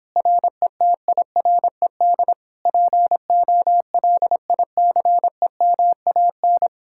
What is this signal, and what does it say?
Morse code, heard as a keyed tone, about 29 wpm, 700 Hz. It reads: RETIRED POLICEMAN